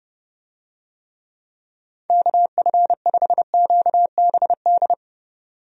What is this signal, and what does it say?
Morse code, heard as a keyed tone, about 32 wpm, 700 Hz. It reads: KF5QBD
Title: USA Call Signs - 6 Characters - 30wpm